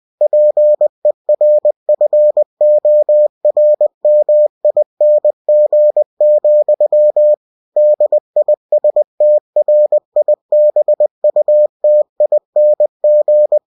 Morse code, heard as a keyed tone, about 20 wpm, 600 Hz. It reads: PERFORMING, DISTRIBUTING
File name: cw-books / winnie-the-pooh / winnie-20wpm-page-0114